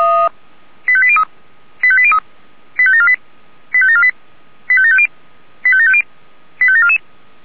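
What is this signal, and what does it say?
Five-tone selective call (ZVEI), heard as quick runs of five tones, tones of about 70 ms. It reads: 85803 85803 85748 85748 85750 85750 85744
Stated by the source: Received by SDR, ZVEI five-tone call.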